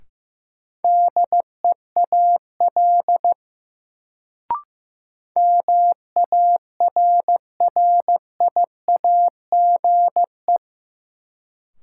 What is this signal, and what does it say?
Morse code, heard as a keyed tone, about 15 wpm, 700 Hz. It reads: DEAL MARRIAGE
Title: Sentences from Top 800 Words - New Word Drill - 15wpm